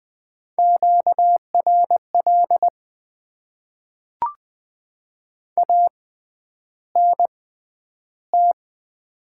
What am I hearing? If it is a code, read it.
Morse code, heard as a keyed tone, about 20 wpm, spaced as for 10 wpm, 700 Hz. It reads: QRL ANT
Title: Top 20 QSO Elements - 6x Character Spacing - 20wpm